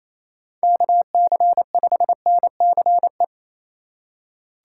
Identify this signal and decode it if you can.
Morse code, heard as a keyed tone, about 28 wpm, 700 Hz. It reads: KC5NCE
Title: USA Call Signs - 6 Characters - 2x Character Spacing - 28wpm